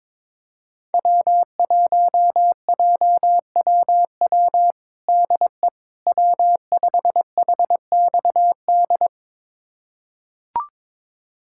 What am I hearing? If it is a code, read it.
Morse code, heard as a keyed tone, about 21 wpm, 700 Hz. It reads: W1JWW DE W5HXD
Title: USA Callsign DE Callsign 22wpm